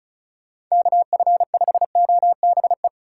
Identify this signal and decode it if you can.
Morse code, heard as a keyed tone, about 34 wpm, 700 Hz. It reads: KF5OBE
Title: USA Call Signs - 6 Characters - 3x Character Spacing - 35wpm